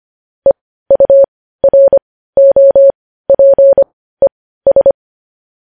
Morse code, heard as a keyed tone, about 25 wpm, spaced as for 15 wpm, 550 Hz. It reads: EUROPES
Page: News Headlines in Morse Code at 15 WPM